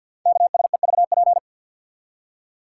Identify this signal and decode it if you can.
Morse code, heard as a keyed tone, about 51 wpm, 700 Hz. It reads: KSEVP
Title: Code Groups - 5 Letter:Numbers - 50wpm